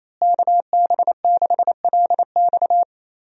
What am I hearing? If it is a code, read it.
Morse code, heard as a keyed tone, about 28 wpm, 700 Hz. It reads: KB6LX
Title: USA Call Signs - 5 Characters - Repeated 2x - 28wpm